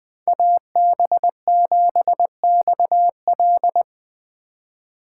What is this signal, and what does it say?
Morse code, heard as a keyed tone, about 20 wpm, 700 Hz. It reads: AB7XL